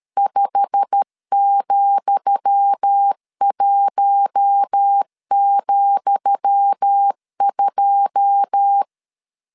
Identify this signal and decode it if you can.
Morse code, heard as a keyed tone, about 13 wpm, 800 Hz. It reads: 5,1,2
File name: LEZIONI / 10 SIGNS / 10D COMA 12wpm 2544